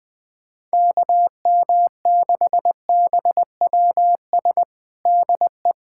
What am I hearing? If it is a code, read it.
Morse code, heard as a keyed tone, about 20 wpm, 700 Hz. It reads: KM6BWS DE
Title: USA Callsign DE Callsign - 6 Characters - 20wpm